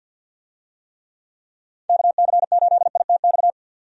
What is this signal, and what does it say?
Morse code, heard as a keyed tone, about 51 wpm, 700 Hz. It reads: KC8ITX
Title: USA Call Signs - 6 Characters - 50wpm